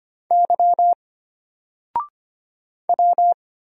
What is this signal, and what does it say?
Morse code, heard as a keyed tone, about 25 wpm, 700 Hz. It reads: Y W
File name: Single-Letter-Number-25wpm